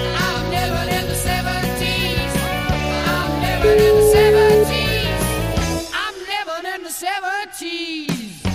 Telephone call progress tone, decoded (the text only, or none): ringback tone